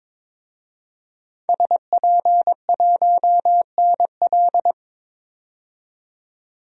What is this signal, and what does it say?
Morse code, heard as a keyed tone, about 22 wpm, 700 Hz. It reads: SP1NL
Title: DXCC-100 Call Signs - 5 Characters - 22wpm